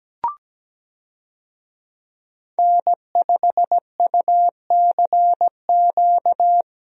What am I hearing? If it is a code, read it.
Morse code, heard as a keyed tone, about 17 wpm, 700 Hz. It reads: N5UCQ
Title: USA Call Signs - 5 Characters - Repeated 3x - 17wpm